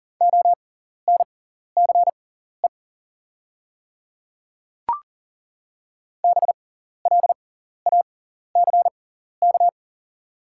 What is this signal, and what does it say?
Morse code, heard as a keyed tone, about 38 wpm, spaced as for 14 wpm, 700 Hz. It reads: ONCE BLACK
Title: 6x Character Spacing - Top 500 Words - 40wpm